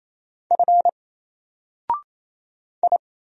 Morse code, heard as a keyed tone, about 27 wpm, 700 Hz. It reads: F I
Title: Single Character (Letter:Number:Punctuation:Prosign) Alt 28wpm